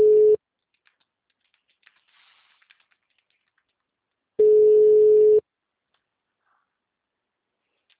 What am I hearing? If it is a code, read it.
Ringback tone (Europe, 425 Hz).